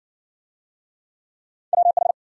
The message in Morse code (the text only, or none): WH